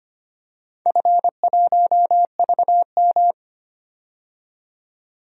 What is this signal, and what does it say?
Morse code, heard as a keyed tone, about 25 wpm, 700 Hz. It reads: F1VM
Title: DXCC-100 Call Signs - 4 Characters - 25wpm